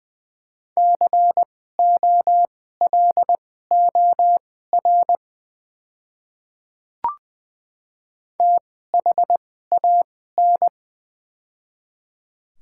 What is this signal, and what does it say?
Morse code, heard as a keyed tone, about 21 wpm, spaced as for 15 wpm, 700 Hz. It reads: COLOR THAN
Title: 2x Character Spacing - Top 300 Words - 20wpm